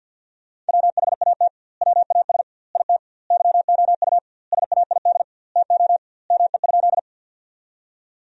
Morse code, heard as a keyed tone, about 50 wpm, 700 Hz. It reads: WHAT WAS IT YOU SAID TO ME?
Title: 2x Word-Spacing - Sentences from Top 200 Words - 50wpm